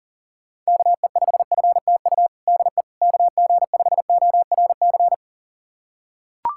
Morse code, heard as a keyed tone, about 40 wpm, 700 Hz. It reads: KE5FTU DE KG5ORC
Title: USA Callsign DE Callsign 40wpm